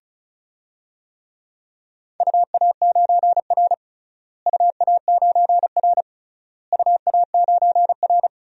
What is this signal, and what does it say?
Morse code, heard as a keyed tone, about 36 wpm, 700 Hz. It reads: UA9R UA9R UA9R
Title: DXCC-100 Call Signs - 4 Characters - Repeated 3x - 35wpm